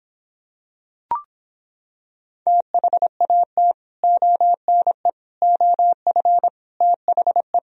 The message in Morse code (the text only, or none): THAT ONE OF THE